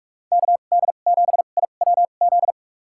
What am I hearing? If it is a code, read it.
Morse code, heard as a keyed tone, about 46 wpm, spaced as for 32 wpm, 700 Hz. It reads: KD7IWZ
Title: USA Call Signs - 6 Characters - 2x Character Spacing - 45wpm